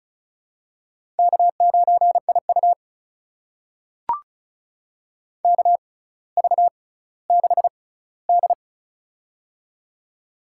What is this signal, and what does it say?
Morse code, heard as a keyed tone, about 34 wpm, spaced as for 18 wpm, 700 Hz. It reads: K9IU KV6D
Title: USA Call Signs - 4 Characters - 6x Character Spacing - 35wpm